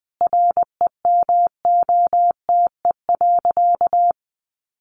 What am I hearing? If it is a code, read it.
Morse code, heard as a keyed tone, about 19 wpm, 700 Hz. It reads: REMOTE.
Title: Sentences from Top 1300 Words 20wpm